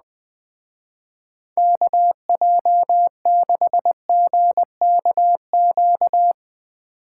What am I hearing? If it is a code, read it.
Morse code, heard as a keyed tone, about 19 wpm, 700 Hz. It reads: KJ6GKQ
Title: Speed Racing - USA 2x3 Call Signs 17wpm